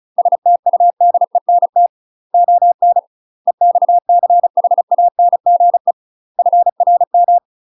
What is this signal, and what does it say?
Morse code, heard as a keyed tone, about 36 wpm, 700 Hz. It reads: STUDENT ON EXCHANGE FRM
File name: UK MASTER QSOs TRAINING / 35WPM / UK MASTER QSOs 0038